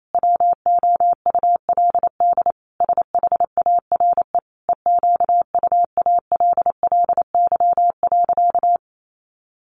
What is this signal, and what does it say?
Morse code, heard as a keyed tone, about 28 wpm, 700 Hz. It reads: WOULD SHARE EQUALLY.
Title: Sentences from Top 1400 Words 28wpm